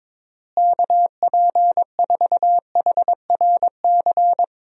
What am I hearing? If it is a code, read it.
Morse code, heard as a keyed tone, about 22 wpm, 700 Hz. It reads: KP4HRC
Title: USA Call Signs - 6 Characters - Repeated 2x - 22wpm